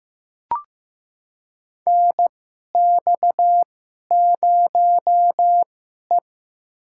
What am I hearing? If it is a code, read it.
Morse code, heard as a keyed tone, about 15 wpm, spaced as for 11 wpm, 700 Hz. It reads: NX0E
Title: USA Call Signs - 4 Characters - 2x Character Spacing - 15wpm